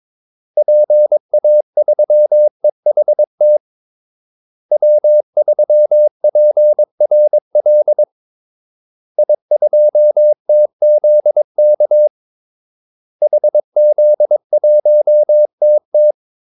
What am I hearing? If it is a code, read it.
Morse code, heard as a keyed tone, about 22 wpm, 600 Hz. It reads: PA3EHT W3PRL I2TZK HZ1TT
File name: Callsigns-22wpm